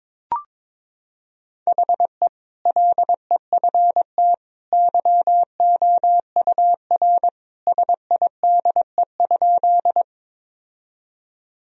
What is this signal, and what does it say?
Morse code, heard as a keyed tone, about 22 wpm, 700 Hz. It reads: HE LEFT YOUR SIDE?
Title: Sentences from Top 1000 Words - Review Part 1 - 22wpm